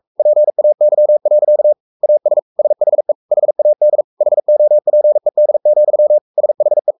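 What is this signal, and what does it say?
Morse code, heard as a keyed tone, about 44 wpm, 600 Hz. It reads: PAY. AS SHE HAD HOPED, SHE
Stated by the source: Morse code at 43 wpm sending Ken Follett's novel Never in English, from CW med Gnister.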